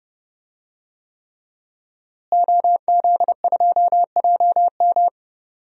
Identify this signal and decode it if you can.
Morse code, heard as a keyed tone, about 31 wpm, 700 Hz. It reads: OZ2JM
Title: DXCC-100 Call Signs - 5 Characters - Repeated 3x - 30wpm